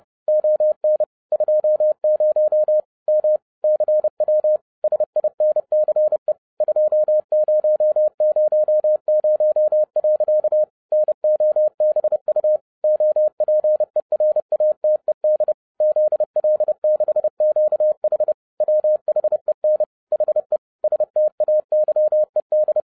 Morse code, heard as a keyed tone, about 30 wpm, 610 Hz. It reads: ON 20 M CW SINCE 2000. NOBU OPERATED ZL6QH WHEN HE STAYED
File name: 30wpm